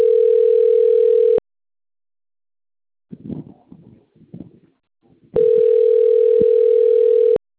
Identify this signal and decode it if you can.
Ringback tone (North America), 440 + 480 Hz.